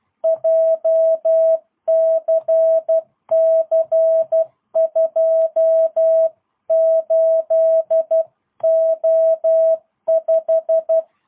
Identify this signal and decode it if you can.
Morse code, heard as a keyed tone, about 12 wpm, 660 Hz. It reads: JCC 28O5
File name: jh8cbh / 2022cw